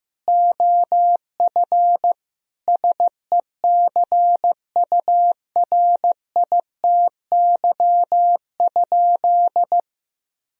Morse code, heard as a keyed tone, about 16 wpm, 700 Hz. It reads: OF SECURITY?
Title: Sentences from Top 1000 Words - Review Part 4 - 15wpm